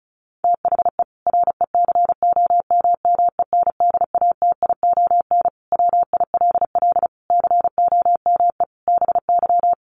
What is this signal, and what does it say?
Morse code, heard as a keyed tone, about 34 wpm, 700 Hz. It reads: THE RECOMMENDATION WILL COME BY
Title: Sentences from Top 1400 Words 35wpm